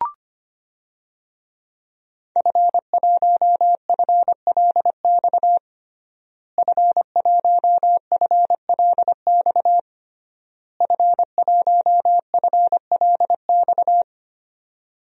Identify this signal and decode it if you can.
Morse code, heard as a keyed tone, about 25 wpm, 700 Hz. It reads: F1FLX F1FLX F1FLX
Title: DXCC-100 Call Signs - 5 Characters - Repeated 3x - 25wpm